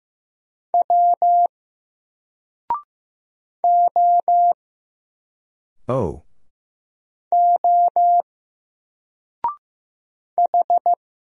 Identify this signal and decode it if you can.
Morse code, heard as a keyed tone, about 15 wpm, 700 Hz. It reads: W O O H
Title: Single Letter-Number 15wpm